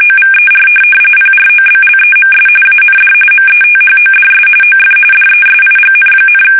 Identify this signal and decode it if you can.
RTTY, about 45.45 baud. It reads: OVER PROBLEMS AND METHODS OF DESIGN,
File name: fsk850 / benjamin-g-lamme